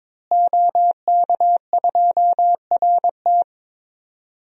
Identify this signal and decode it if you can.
Morse code, heard as a keyed tone, about 21 wpm, 700 Hz. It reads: OK2RT
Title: DXCC-100 Call Signs - 5 Characters - Repeated 3x - 22wpm